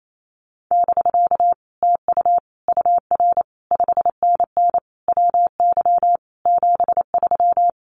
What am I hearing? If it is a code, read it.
Morse code, heard as a keyed tone, about 28 wpm, 700 Hz. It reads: <BK> TU UR 5NN WY 73
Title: Common QSO Phrases Encore 28wpm